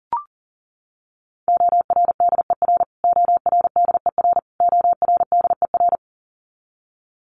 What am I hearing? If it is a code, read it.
Morse code, heard as a keyed tone, about 40 wpm, 700 Hz. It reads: ORDER ORDER ORDER
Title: Top 500 Words Repeated 40wpm